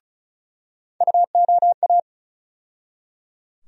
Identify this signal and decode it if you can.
Morse code, heard as a keyed tone, about 36 wpm, 700 Hz. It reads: UOA
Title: USA Call Sign Postfixes - 3 Letters 35wpm